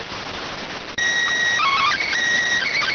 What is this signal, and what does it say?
SSTV (slow-scan television), PD 290.